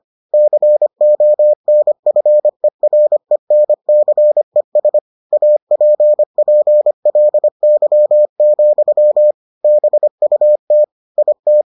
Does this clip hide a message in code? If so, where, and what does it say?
Yes: Morse code from 0.3 to 11.6 s, heard as a keyed tone, about 25 wpm, 600 Hz. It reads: CONFERENCES APPLY, BUT IT